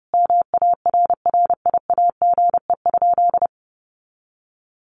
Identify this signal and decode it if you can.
Morse code, heard as a keyed tone, about 30 wpm, 700 Hz. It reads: MARRIAGE?
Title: Sentences from Top 1400 Words 30wpm